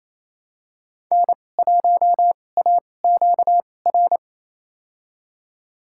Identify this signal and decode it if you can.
Morse code, heard as a keyed tone, about 28 wpm, spaced as for 20 wpm, 700 Hz. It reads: N1AQR
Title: USA Call Signs - 5 Characters - 2x Character Spacing - 28wpm